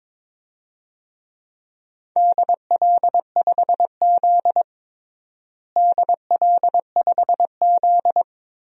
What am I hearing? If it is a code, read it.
Morse code, heard as a keyed tone, about 22 wpm, 700 Hz. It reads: DL5Z DL5Z
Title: DXCC-100 Call Signs - 4 Characters - Repeated 2x - 22wpm